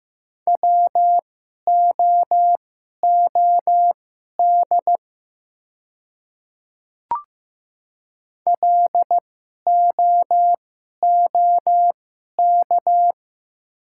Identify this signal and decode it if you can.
Morse code, heard as a keyed tone, about 15 wpm, spaced as for 11 wpm, 700 Hz. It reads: WOOD LOOK
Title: 2x Character Spacing - Top 300 Words - 15wpm